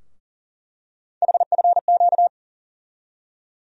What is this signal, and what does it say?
Morse code, heard as a keyed tone, about 39 wpm, 700 Hz. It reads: HFQ
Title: USA Call Sign Postfixes - 3 Letters 40wpm